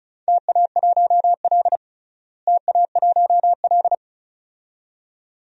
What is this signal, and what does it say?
Morse code, heard as a keyed tone, about 36 wpm, 700 Hz. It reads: TA1L TA1L
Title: DXCC-100 Call Signs - 4 Characters - Repeated 2x - 35wpm